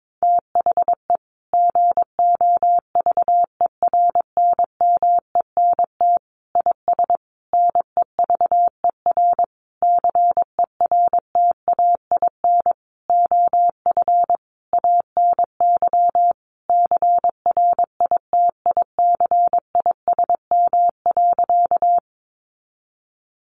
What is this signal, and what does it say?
Morse code, heard as a keyed tone, about 22 wpm, 700 Hz. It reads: THE GOVERNMENT IS NEVER CERTAIN OF ANY CRITICISM.
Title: Sentences from Top 1300 Words 22wpm